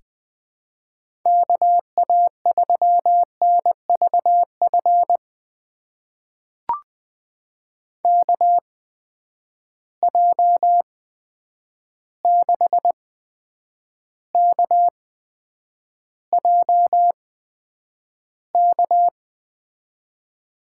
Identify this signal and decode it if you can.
Morse code, heard as a keyed tone, about 21 wpm, spaced as for 9 wpm, 700 Hz. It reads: KA3NVF KJ6KJK
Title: USA Call Signs - 6 Characters - 8x Character Spacing - 20wpm